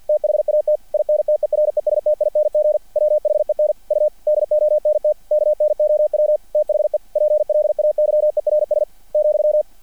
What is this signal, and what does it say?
Morse code, heard as a keyed tone, about 50 wpm, 600 Hz. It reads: THAT INTERESTING WHEN U DONT KNOW THE PLAYERS ,
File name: archived / CRSnet